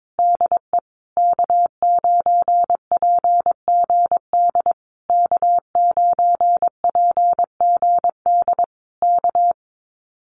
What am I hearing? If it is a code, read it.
Morse code, heard as a keyed tone, about 22 wpm, 700 Hz. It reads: DE K9PGD K9PGD K
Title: Common QSO Phrases Encore 22wpm